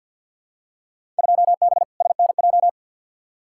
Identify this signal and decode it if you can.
Morse code, heard as a keyed tone, about 45 wpm, 700 Hz. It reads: 2B SNJ
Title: Field Day Exchanges Encore - 50wpm